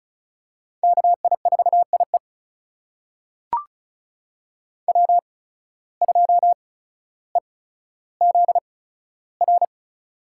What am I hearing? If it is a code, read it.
Morse code, heard as a keyed tone, about 36 wpm, spaced as for 15 wpm, 700 Hz. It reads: KI4IE W2EZR
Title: USA Call Signs - 5 Characters - 8x Character Spacing - 35wpm